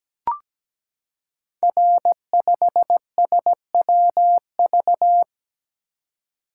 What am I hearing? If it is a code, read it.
Morse code, heard as a keyed tone, about 17 wpm, 700 Hz. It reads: R5SWV
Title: Code Groups - 5 Letter:Numbers - 17wpm